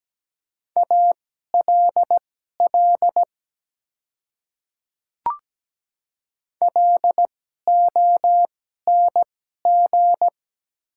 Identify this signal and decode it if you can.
Morse code, heard as a keyed tone, about 17 wpm, spaced as for 12 wpm, 700 Hz. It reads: ALL LONG
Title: 2x Character Spacing - Top 100 Words - 17wpm